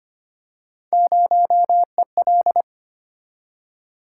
Morse code, heard as a keyed tone, about 25 wpm, 700 Hz. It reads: 0EL
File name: Three Letter-Number Combinations - 25wpm